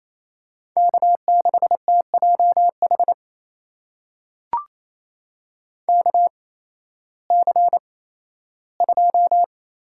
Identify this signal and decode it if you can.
Morse code, heard as a keyed tone, about 27 wpm, spaced as for 15 wpm, 700 Hz. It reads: K6TJH KC2
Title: USA Call Signs - 8x Character Spacing - 28wpm